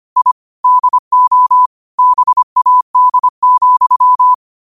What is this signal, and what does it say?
Morse code, heard as a keyed tone, about 25 wpm, 1000 Hz. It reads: I DO BAD,